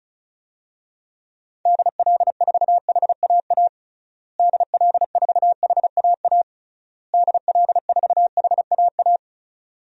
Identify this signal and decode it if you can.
Morse code, heard as a keyed tone, about 36 wpm, 700 Hz. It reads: DL4HAA DL4HAA DL4HAA
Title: DXCC-100 Call Signs - 6 Characters - Repeated 3x - 35wpm